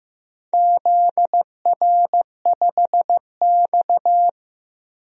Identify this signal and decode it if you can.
Morse code, heard as a keyed tone, about 15 wpm, 700 Hz. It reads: ZR5X
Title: DXCC-100 Call Signs - 4 Characters - Repeated 3x - 15wpm